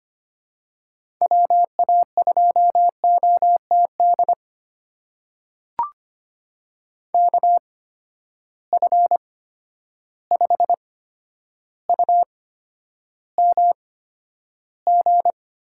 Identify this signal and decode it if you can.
Morse code, heard as a keyed tone, about 25 wpm, spaced as for 11 wpm, 700 Hz. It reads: WA2OTD KF5UMG
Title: USA Call Signs - 6 Characters - 8x Character Spacing - 25wpm